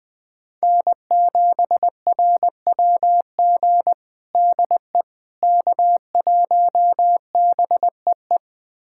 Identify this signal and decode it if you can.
Morse code, heard as a keyed tone, about 20 wpm, 700 Hz. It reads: N7RWG DE K1BEE
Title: USA Callsign DE Callsign - 5 Characters - 20wpm